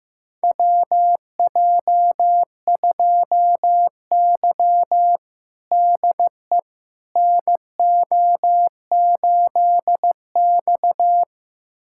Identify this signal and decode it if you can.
Morse code, heard as a keyed tone, about 15 wpm, 700 Hz. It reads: WJ2Y DE NO8X
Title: USA Callsign DE Callsign - 4 Characters - 15wpm